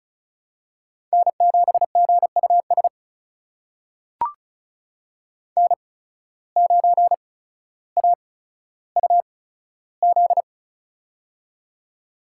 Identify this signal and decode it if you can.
Morse code, heard as a keyed tone, about 36 wpm, spaced as for 15 wpm, 700 Hz. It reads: N7GUS N9AUZ
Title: USA Call Signs - 5 Characters - 8x Character Spacing - 35wpm